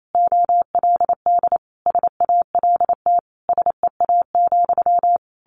Morse code, heard as a keyed tone, about 28 wpm, 700 Hz. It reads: OLD SALT SEA,
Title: A Sound Of Thunder 28wpm